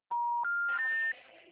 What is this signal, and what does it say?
Special information tone, with noise.